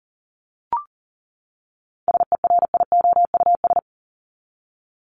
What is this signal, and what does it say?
Morse code, heard as a keyed tone, about 40 wpm, 700 Hz. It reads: SERIOUS
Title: LY Suffix - Top 100 Words with Base Words 40wpm